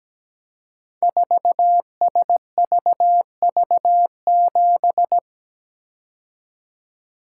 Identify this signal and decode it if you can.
Morse code, heard as a keyed tone, about 17 wpm, 700 Hz. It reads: 4SVV7
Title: Code Groups - 5 Letter:Numbers - 17wpm